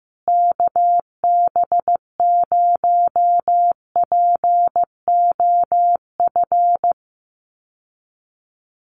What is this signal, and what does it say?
Morse code, heard as a keyed tone, about 15 wpm, 700 Hz. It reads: KB0POF
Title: USA Call Signs Repeated 2x 15wpm